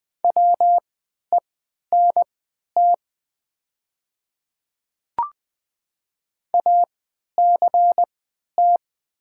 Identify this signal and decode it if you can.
Morse code, heard as a keyed tone, about 20 wpm, spaced as for 11 wpm, 700 Hz. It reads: WENT ACT
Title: 3x Character Spacing - Top 200 Words - 20wpm